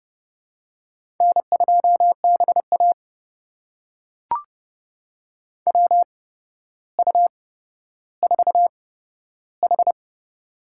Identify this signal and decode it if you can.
Morse code, heard as a keyed tone, about 31 wpm, spaced as for 13 wpm, 700 Hz. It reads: N2BA WU4H
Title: USA Call Signs - 4 Characters - 8x Character Spacing - 30wpm